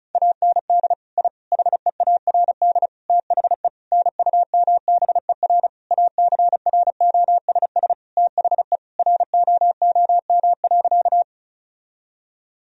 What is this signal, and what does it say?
Morse code, heard as a keyed tone, about 34 wpm, 690 Hz. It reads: AND I HEARD THE NUMBER ACROSS THE ROOM.
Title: Sentences from Top 1000 Words - Review Part 1 - 35wpm